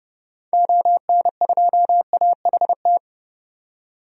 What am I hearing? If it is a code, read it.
Morse code, heard as a keyed tone, about 30 wpm, 700 Hz. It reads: ON2AHT
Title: DXCC-100 Call Signs - 6 Characters - 30wpm